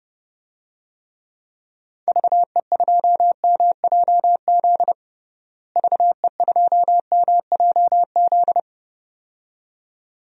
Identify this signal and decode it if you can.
Morse code, heard as a keyed tone, about 32 wpm, 700 Hz. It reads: VE2MJZ VE2MJZ
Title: DXCC-100 Call Signs - 6 Characters - Repeated 2x - 30wpm